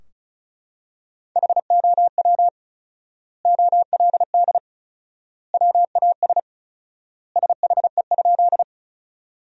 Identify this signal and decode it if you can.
Morse code, heard as a keyed tone, about 36 wpm, 700 Hz. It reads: HOW OLD WAS SHE?
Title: 4x Word-Spacing - Sentences from Top 200 Words - 35wpm